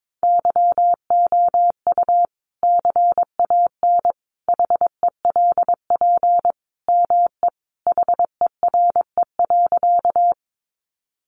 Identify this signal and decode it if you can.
Morse code, heard as a keyed tone, about 22 wpm, 700 Hz. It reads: YOU CAN HELP ME HERE.
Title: Sentences from Top 200 Words 22wpm